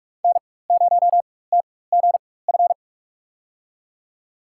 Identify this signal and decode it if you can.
Morse code, heard as a keyed tone, about 46 wpm, spaced as for 21 wpm, 700 Hz. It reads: N0TGF
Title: USA Call Signs - 4x Character Spacing - 45wpm